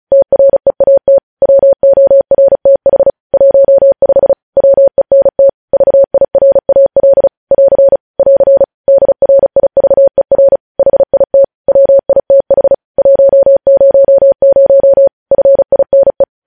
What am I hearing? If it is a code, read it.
Morse code, heard as a keyed tone, about 36 wpm, 550 Hz. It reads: TREAT WORTH 15 WENT VIRAL <AR> <AR> DRIVER HIT WITH 100 FINE
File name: podcast / News headlines-1x-35x35